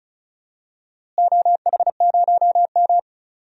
Morse code, heard as a keyed tone, about 34 wpm, 700 Hz. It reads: OH0M
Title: DXCC-100 Call Signs - 4 Characters - Repeated 3x - 35wpm